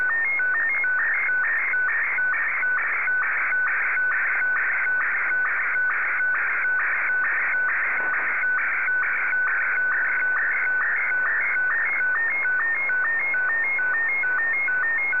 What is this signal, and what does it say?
SSTV (slow-scan television), Martin 1.